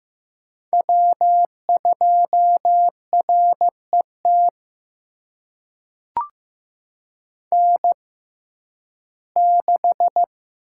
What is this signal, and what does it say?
Morse code, heard as a keyed tone, about 15 wpm, spaced as for 11 wpm, 700 Hz. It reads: W2RET N6
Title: USA Call Signs - 5 Characters - 6x Character Spacing - 15wpm